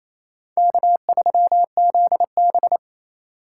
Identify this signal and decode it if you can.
Morse code, heard as a keyed tone, about 36 wpm, 700 Hz. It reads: K3ZB